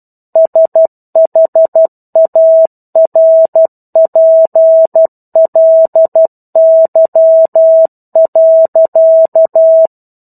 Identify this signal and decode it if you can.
Morse code, heard as a keyed tone, about 12 wpm, 650 Hz. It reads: SHARPLY.
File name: 01-Chapter-01 12 WPM